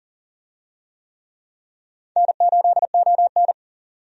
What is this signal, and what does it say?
Morse code, heard as a keyed tone, about 38 wpm, 700 Hz. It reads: N8ON
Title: USA Call Signs - 4 Characters - 40wpm